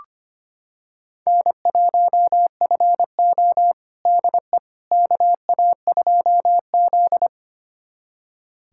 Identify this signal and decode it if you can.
Morse code, heard as a keyed tone, about 25 wpm, 700 Hz. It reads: N1FO DE KA2Z
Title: USA Callsign DE Callsign - 4 Characters - 25wpm